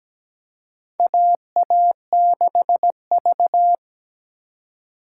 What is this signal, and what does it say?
Morse code, heard as a keyed tone, about 17 wpm, 700 Hz. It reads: AA6V
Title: USA Call Signs - 4 Characters - 3x Character Spacing - 17wpm